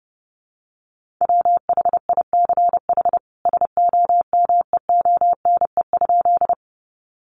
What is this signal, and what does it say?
Morse code, heard as a keyed tone, about 28 wpm, 700 Hz. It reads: WHICH SOMEONE?
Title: Sentences from Top 600 Words 30wpm